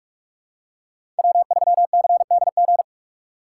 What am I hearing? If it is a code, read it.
Morse code, heard as a keyed tone, about 41 wpm, 700 Hz. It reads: W3CDG
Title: USA Call Signs - 5 Characters - 3x Character Spacing - 45wpm